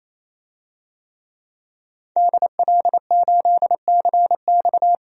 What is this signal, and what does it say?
Morse code, heard as a keyed tone, about 27 wpm, 700 Hz. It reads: DL8CX
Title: DXCC-100 Call Signs - 5 Characters - 28wpm